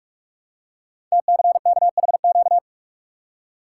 Morse code, heard as a keyed tone, about 44 wpm, 700 Hz. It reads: TCKHQ
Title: Code Groups - 5 Letters - 45wpm